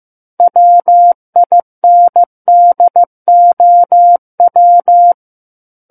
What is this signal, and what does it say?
Morse code, heard as a keyed tone, about 15 wpm, 700 Hz. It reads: WINDOW
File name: Chapter 6 - 15 WPM